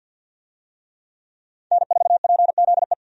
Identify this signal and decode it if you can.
Morse code, heard as a keyed tone, about 51 wpm, 700 Hz. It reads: N4PZE
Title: USA Call Signs - 5 Characters - 50wpm